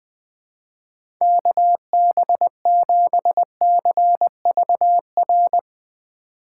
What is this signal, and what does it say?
Morse code, heard as a keyed tone, about 20 wpm, 700 Hz. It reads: KB7CVR
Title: USA Call Signs - 6 Characters - 4x Character Spacing - 20wpm